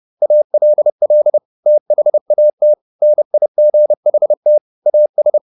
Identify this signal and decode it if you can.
Morse code, heard as a keyed tone, about 30 wpm, 600 Hz. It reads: ALL THAT NIGHT AS
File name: the old man and the sea / the old man and the sea 1 30wpm